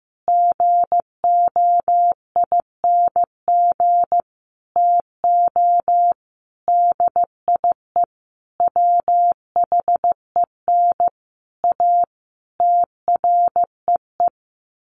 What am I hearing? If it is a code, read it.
Morse code, heard as a keyed tone, about 15 wpm, 700 Hz. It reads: GOING TO DIE WHEN A TREE
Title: A Sound Of Thunder 15wpm